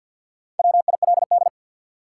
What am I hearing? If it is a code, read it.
Morse code, heard as a keyed tone, about 49 wpm, 700 Hz. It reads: WILD
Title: Four Letter Words - 50wpm